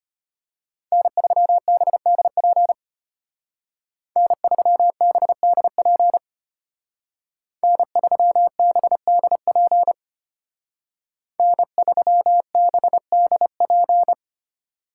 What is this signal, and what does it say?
Morse code, heard as a keyed tone, about 31 wpm, 700 Hz. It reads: N3BDP N3BDP N3BDP N3BDP